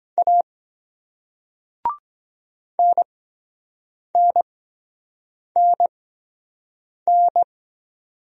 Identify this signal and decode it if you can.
Morse code, heard as a keyed tone, about 23 wpm, spaced as for 7 wpm, 700 Hz. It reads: A NNNN